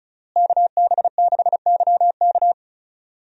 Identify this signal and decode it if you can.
Morse code, heard as a keyed tone, about 36 wpm, 700 Hz. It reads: KB6YK
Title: USA Call Signs - 5 Characters - Repeated 3x - 35wpm